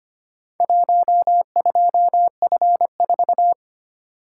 Morse code, heard as a keyed tone, about 25 wpm, 700 Hz. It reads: 12F4
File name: practice / 25wpm / Four Letter-Number Combinations - 25wpm